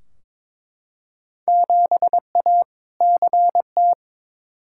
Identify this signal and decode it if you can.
Morse code, heard as a keyed tone, about 22 wpm, 700 Hz. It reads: 7A CT
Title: Field Day Exchanges - 22wpm